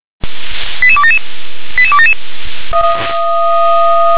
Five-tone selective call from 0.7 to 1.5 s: 88277

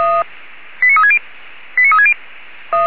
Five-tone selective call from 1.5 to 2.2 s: 89360

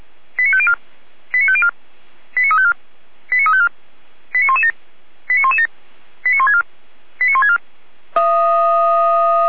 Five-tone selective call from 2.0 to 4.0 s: 89354 89354

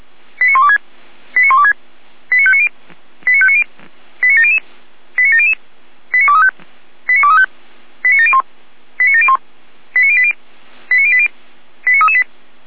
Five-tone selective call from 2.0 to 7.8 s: 89590 89590 89700 89700 89235 89235

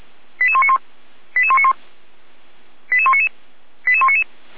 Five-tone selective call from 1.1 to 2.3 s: 88191

